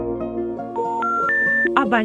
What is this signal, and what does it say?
Special information tone, with speech.